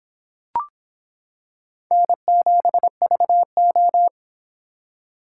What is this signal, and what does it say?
Morse code, heard as a keyed tone, about 25 wpm, 700 Hz. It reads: N7VO